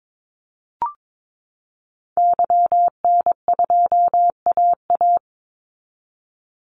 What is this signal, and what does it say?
Morse code, heard as a keyed tone, about 22 wpm, 700 Hz. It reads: YN2AA
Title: Call Signs Repeated 3x - 22wpm